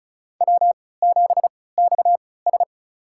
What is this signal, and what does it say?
Morse code, heard as a keyed tone, about 34 wpm, spaced as for 20 wpm, 700 Hz. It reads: W7XS
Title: USA Call Signs - 4 Characters - 3x Character Spacing - 35wpm